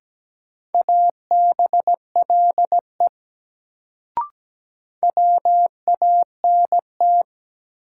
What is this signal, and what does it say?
Morse code, heard as a keyed tone, about 17 wpm, 700 Hz. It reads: ABLE WANT
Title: Sets of 2 Words - New Word Drill - 17wpm